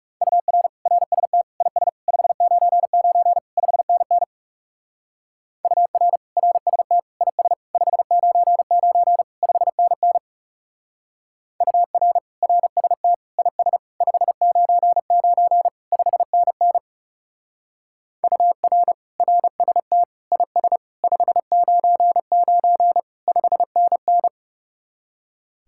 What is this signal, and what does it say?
Morse code, heard as a keyed tone, about 37 wpm, 700 Hz. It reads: UR RST IS 599 5NN UR RST IS 599 5NN UR RST IS 599 5NN UR RST IS 599 5NN
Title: Speed-Racing Common QSO Phrases Encore - 30wpm